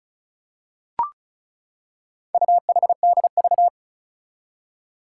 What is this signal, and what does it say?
Morse code, heard as a keyed tone, about 36 wpm, 700 Hz. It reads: UHDV
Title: Four Challenging Letter-Number Combinations 35wpm